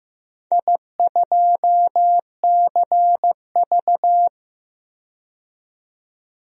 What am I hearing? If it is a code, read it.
Morse code, heard as a keyed tone, about 15 wpm, 700 Hz. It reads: I2CV
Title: DXCC-100 Call Signs - 4 Characters - 15wpm